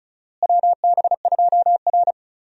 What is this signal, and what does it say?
Morse code, heard as a keyed tone, about 36 wpm, 700 Hz. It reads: WB2R
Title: USA Call Signs - 4 Characters - Repeated 3x - 35wpm